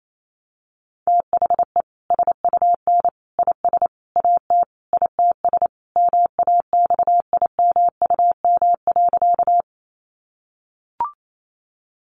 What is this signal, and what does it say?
Morse code, heard as a keyed tone, about 28 wpm, 700 Hz. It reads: THE SUN IS AT ITS MAXIMUM.
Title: Sentences from Top 1300 Words 28wpm